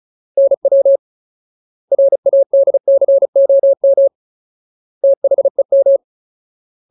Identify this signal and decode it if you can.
Morse code, heard as a keyed tone, about 34 wpm, 550 Hz. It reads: NW RADCOM THEM
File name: Ultimate Word List / 35WPM Ultimate